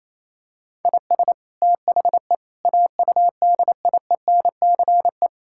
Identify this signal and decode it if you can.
Morse code, heard as a keyed tone, about 27 wpm, 700 Hz. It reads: IS THE AUDIENCE
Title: Sentences from Top 1000 Words - Review Part 5 - 28wpm